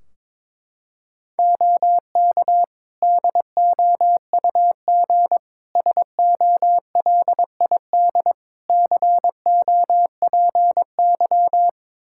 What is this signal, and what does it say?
Morse code, heard as a keyed tone, about 22 wpm, 700 Hz. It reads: OK DOUG SOLID COPY
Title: Practice QSO - Exchange 3 - More Info - 22wpm